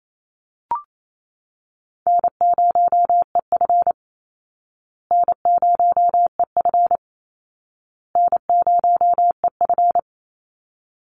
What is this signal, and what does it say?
Morse code, heard as a keyed tone, about 28 wpm, 700 Hz. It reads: N0EF N0EF N0EF
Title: Easy Call Signs Repeated 3x - 28wpm